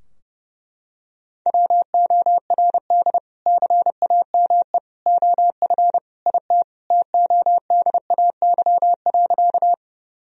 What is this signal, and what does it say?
Morse code, heard as a keyed tone, about 29 wpm, 700 Hz. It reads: WORD CAME OF IT TODAY.
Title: Sentences from Top 1000 Words - Review Part 4 - 30wpm